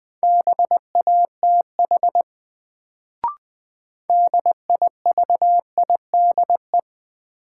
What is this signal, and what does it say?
Morse code, heard as a keyed tone, about 21 wpm, 700 Hz. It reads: BATH DIVIDE
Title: Sentences from Top 1600 Words - New Word Drill Alt - 20wpm